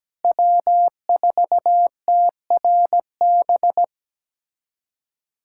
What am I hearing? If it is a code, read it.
Morse code, heard as a keyed tone, about 17 wpm, 700 Hz. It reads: W4TRB
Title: USA Callsign DE Callsign - 5 Characters - 17wpm